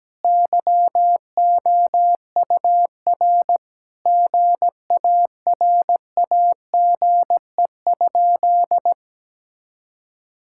Morse code, heard as a keyed tone, about 17 wpm, 700 Hz. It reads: YOUR GARAGE?
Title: Sentences from Top 1600 Words 17wpm